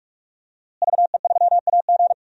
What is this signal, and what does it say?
Morse code, heard as a keyed tone, about 42 wpm, 700 Hz. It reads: VE3AG
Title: DXCC-100 Call Signs - 5 Characters - Repeated 2x - 45wpm